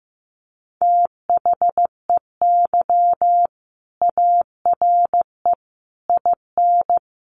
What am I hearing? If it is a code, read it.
Morse code, heard as a keyed tone, about 15 wpm, 700 Hz. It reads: THEY ARE IN
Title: Sentences from Top 1000 Words 15wpm